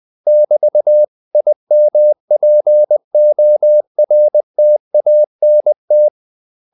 Morse code, heard as a keyed tone, about 20 wpm, 600 Hz. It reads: <BT>IMPORTANT
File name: CW / Practice / 500MostCommonEnglishWords20